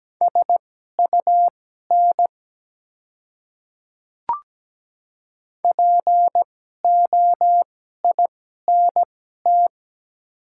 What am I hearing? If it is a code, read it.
Morse code, heard as a keyed tone, about 17 wpm, spaced as for 12 wpm, 690 Hz. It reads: SUN POINT
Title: 2x Character Spacing - Top 300 Words - 17wpm